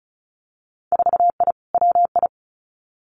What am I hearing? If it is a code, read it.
Morse code, heard as a keyed tone, about 36 wpm, 700 Hz. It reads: 4I WI
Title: Winter Field Day Exchanges 35wpm